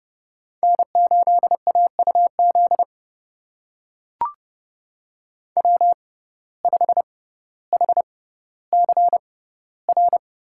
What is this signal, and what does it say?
Morse code, heard as a keyed tone, about 30 wpm, spaced as for 15 wpm, 700 Hz. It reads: N8AUZ W5HCR
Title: USA Call Signs - 6x Character Spacing - 30wpm